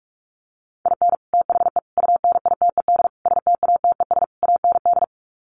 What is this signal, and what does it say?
Morse code, heard as a keyed tone, about 44 wpm, 700 Hz. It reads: IN THE UNITED STATES AND
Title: Sets of 5 Words 45wpm